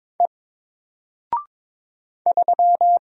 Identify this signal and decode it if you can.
Morse code, heard as a keyed tone, about 22 wpm, 700 Hz. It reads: E 3
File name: Single-Letter-Number-22wpm